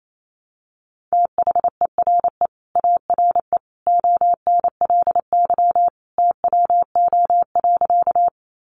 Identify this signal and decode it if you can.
Morse code, heard as a keyed tone, about 28 wpm, 700 Hz. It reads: THERE ARE ONLY TWO.
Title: Sentences from Top 200 Words 28wpm